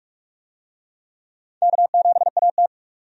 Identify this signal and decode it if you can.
Morse code, heard as a keyed tone, about 46 wpm, 700 Hz. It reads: K7AT
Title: USA Call Signs - 4 Characters - 45wpm